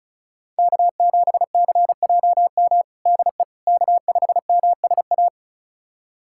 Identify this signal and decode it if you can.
Morse code, heard as a keyed tone, about 36 wpm, 700 Hz. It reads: K7CJM DE K5MSA
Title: USA Callsign DE Callsign - 5 Characters - 35wpm